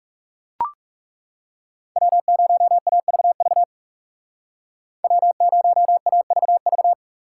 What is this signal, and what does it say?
Morse code, heard as a keyed tone, about 38 wpm, 700 Hz. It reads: W0AVV W0AVV